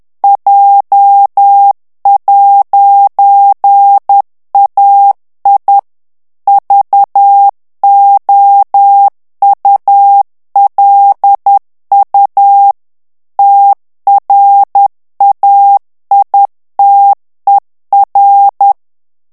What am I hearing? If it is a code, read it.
Morse code, heard as a keyed tone, about 10 wpm, 800 Hz. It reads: J'AI VOULU TRAITER
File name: FRANCAIS TEXTE CLAIR 2 VITESSE 10 MOTS MINUTES